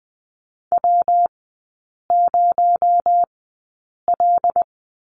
Morse code, heard as a keyed tone, about 20 wpm, spaced as for 8 wpm, 700 Hz. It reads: W0L
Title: 218 Lesson 35e - Number 0 - Three Characters 20wpm